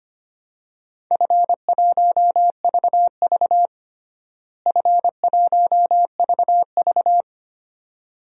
Morse code, heard as a keyed tone, about 25 wpm, 700 Hz. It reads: F1VV F1VV
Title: DXCC-100 Call Signs - 4 Characters - Repeated 2x - 25wpm